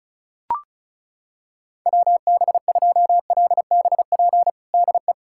Morse code, heard as a keyed tone, about 36 wpm, 700 Hz. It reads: WB2LBP DE
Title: USA Callsign DE Callsign - 6 Characters - 35wpm